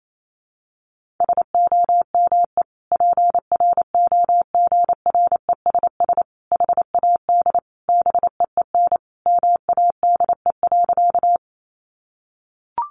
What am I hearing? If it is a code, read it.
Morse code, heard as a keyed tone, about 28 wpm, 700 Hz. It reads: SOME PROGRESS HAD BEEN MADE.
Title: Sentences from Top 1400 Words 28wpm